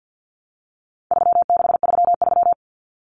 Morse code, heard as a keyed tone, about 49 wpm, 700 Hz. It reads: 3633
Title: Numbers - 4-Digit 50wpm